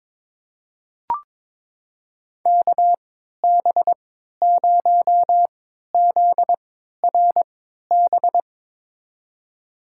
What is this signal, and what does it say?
Morse code, heard as a keyed tone, about 22 wpm, spaced as for 12 wpm, 700 Hz. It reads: KB0ZRB
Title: USA Call Signs - 6 Characters - 3x Character Spacing - 22wpm